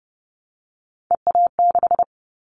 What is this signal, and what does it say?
Morse code, heard as a keyed tone, about 30 wpm, 700 Hz. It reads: EA6
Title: Callsign Prefixes for Top 50 DXCC - 30wpm